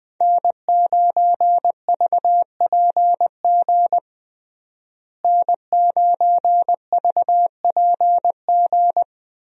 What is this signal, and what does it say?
Morse code, heard as a keyed tone, about 20 wpm, 700 Hz. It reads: N9VPG N9VPG
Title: USA Call Signs - 5 Characters - Repeated 2x - 20wpm